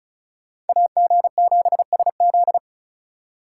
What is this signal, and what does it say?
Morse code, heard as a keyed tone, about 34 wpm, 700 Hz. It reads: AG7SZ
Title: USA Call Signs - 5 Characters - 3x Character Spacing - 35wpm